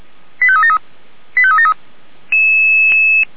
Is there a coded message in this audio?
Yes: five-tone selective call (ZVEI) from 0.4 to 1.7 s, heard as quick runs of five tones, tones of about 70 ms. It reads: 85373 85373